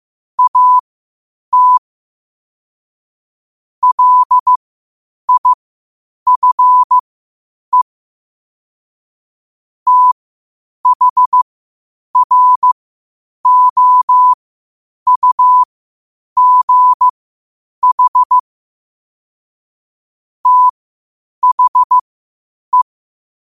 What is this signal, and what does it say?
Morse code, heard as a keyed tone, about 15 wpm, spaced as for 8 wpm, 1000 Hz. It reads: AT LIFE THROUGH THE
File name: qotdfiles / 8wpm